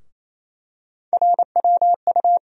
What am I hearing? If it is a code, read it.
Morse code, heard as a keyed tone, about 28 wpm, 700 Hz. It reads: RWU
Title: USA Call Sign Postfixes - 3 Letters 28wpm